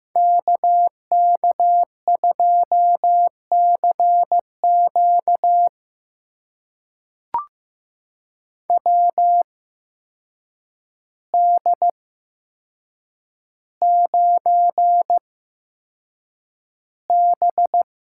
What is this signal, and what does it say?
Morse code, heard as a keyed tone, about 15 wpm, spaced as for 7 wpm, 700 Hz. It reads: KK2CQ WD9B
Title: USA Call Signs - 8x Character Spacing - 15wpm